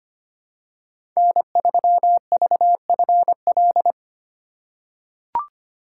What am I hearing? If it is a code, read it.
Morse code, heard as a keyed tone, about 25 wpm, 700 Hz. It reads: N3VFL
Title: USA Call Signs - 5 Characters - 3x Character Spacing - 25wpm